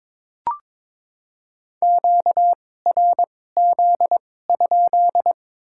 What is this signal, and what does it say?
Morse code, heard as a keyed tone, about 21 wpm, spaced as for 16 wpm, 700 Hz. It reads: QRZ?
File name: Top 90 QSO Elements - 2x Character Spacing - 22wpm